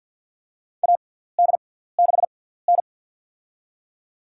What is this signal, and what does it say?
Morse code, heard as a keyed tone, about 53 wpm, spaced as for 17 wpm, 700 Hz. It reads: AD6N
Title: USA Call Signs - 4 Characters - 6x Character Spacing - 50wpm